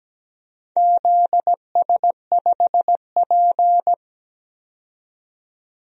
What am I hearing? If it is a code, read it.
Morse code, heard as a keyed tone, about 17 wpm, 700 Hz. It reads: ZS5P
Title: DXCC-100 Call Signs - 4 Characters - 17wpm